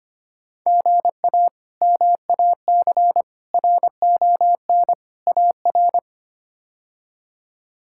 Morse code, heard as a keyed tone, about 25 wpm, 700 Hz. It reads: GA MAC RON AR
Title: K1USN SST QSO Practice - 25wpm